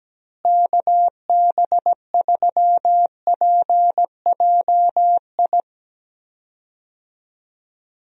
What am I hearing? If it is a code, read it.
Morse code, heard as a keyed tone, about 17 wpm, 700 Hz. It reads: KB3PJI
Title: USA Call Signs - 6 Characters - Repeated 2x - 17wpm